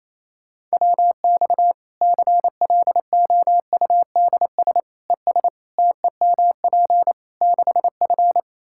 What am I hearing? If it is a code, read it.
Morse code, heard as a keyed tone, about 28 wpm, 700 Hz. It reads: WX CLOUDS ES TEMP 6F
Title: Practice QSO - Exchange 2 - Info - 28wpm